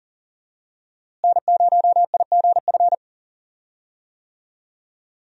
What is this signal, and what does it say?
Morse code, heard as a keyed tone, about 40 wpm, 700 Hz. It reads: N0IGF
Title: USA Call Signs - 5 Characters - Repeated 3x - 40wpm